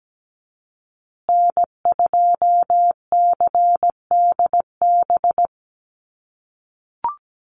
Morse code, heard as a keyed tone, about 17 wpm, 700 Hz. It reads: N2CDB
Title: USA Call Signs Repeated 2x 17wpm